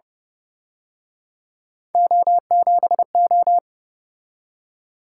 Morse code, heard as a keyed tone, about 31 wpm, 700 Hz. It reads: O7O